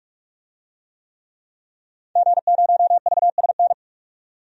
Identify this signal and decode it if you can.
Morse code, heard as a keyed tone, about 42 wpm, 700 Hz. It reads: G0VSN
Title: DXCC-100 Call Signs - 5 Characters - 45wpm